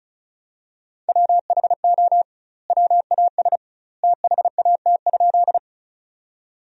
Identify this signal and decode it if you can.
Morse code, heard as a keyed tone, about 36 wpm, 700 Hz. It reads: WHO WAS THAT?
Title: 2x Word-Spacing - Sentences from Top 100 Words - 35wpm